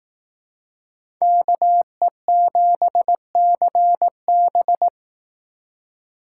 Morse code, heard as a keyed tone, about 18 wpm, 700 Hz. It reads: KE7CB
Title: Speed Racing - USA Call Signs 15wpm